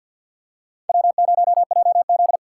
Morse code, heard as a keyed tone, about 50 wpm, 700 Hz. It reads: W0JZ
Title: USA Call Signs - 4 Characters - 6x Character Spacing - 50wpm